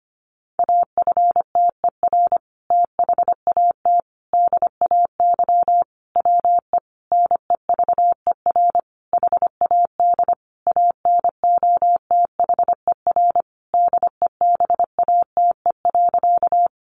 Morse code, heard as a keyed tone, about 25 wpm, 700 Hz. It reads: AFTER THAT DAY WE NEVER HAD ANOTHER DEBATE.
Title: Sentences from Top 1100 Words 25wpm